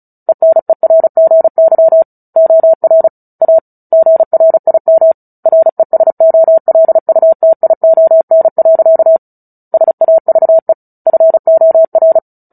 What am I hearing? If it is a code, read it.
Morse code, heard as a keyed tone, about 34 wpm, 650 Hz. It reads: ENERGY OR A GRIM RESOLUTION. SAVE FOR
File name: Book Two / 09-Chapter 9 - 35 WPM